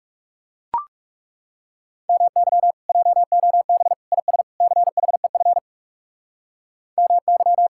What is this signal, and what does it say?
Morse code, heard as a keyed tone, about 36 wpm, 700 Hz. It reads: MY JOB IS CHEF MY
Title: Speed-Racing Common QSO Phrases - 30wpm